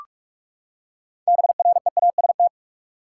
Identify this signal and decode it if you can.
Morse code, heard as a keyed tone, about 47 wpm, 700 Hz. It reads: BREAST
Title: Six Letter Words - 45wpm